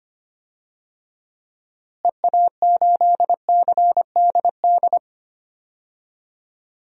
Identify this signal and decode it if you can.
Morse code, heard as a keyed tone, about 25 wpm, 700 Hz. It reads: EA8CDD
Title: DXCC-100 Call Signs - 6 Characters - Repeated 3x - 25wpm